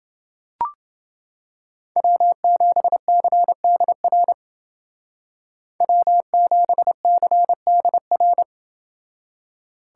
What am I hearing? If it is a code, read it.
Morse code, heard as a keyed tone, about 29 wpm, 700 Hz. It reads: W7CDR W7CDR